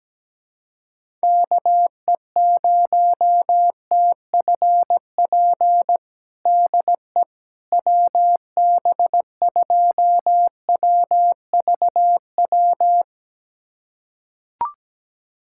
Morse code, heard as a keyed tone, about 17 wpm, 700 Hz. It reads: KE0TFP DE WB2WVW
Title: USA Callsign DE Callsign - 6 Characters - 17wpm